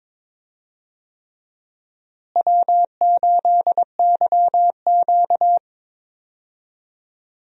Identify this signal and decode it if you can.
Morse code, heard as a keyed tone, about 22 wpm, 700 Hz. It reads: W8YQ
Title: USA Call Signs - 4 Characters - Repeated 3x - 22wpm